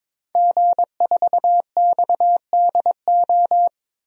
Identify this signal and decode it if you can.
Morse code, heard as a keyed tone, about 21 wpm, 700 Hz. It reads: G4XDO
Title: DXCC-100 Call Signs - 5 Characters - 22wpm